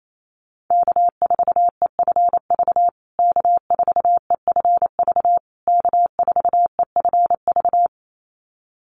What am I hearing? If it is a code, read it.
Morse code, heard as a keyed tone, about 28 wpm, 700 Hz. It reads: K4EFV K4EFV K4EFV
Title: USA Call Signs Repeated 3x 28wpm